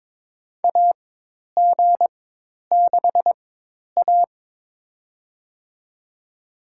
Morse code, heard as a keyed tone, about 22 wpm, spaced as for 10 wpm, 700 Hz. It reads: AG6A
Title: USA Call Signs - 4 Characters - 4x Character Spacing - 22wpm